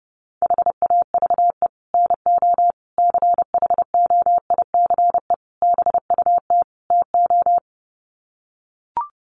Morse code, heard as a keyed tone, about 30 wpm, 700 Hz. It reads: HAVE NO CHOICE BUT TO
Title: Sets of 5 Words Encore 30wpm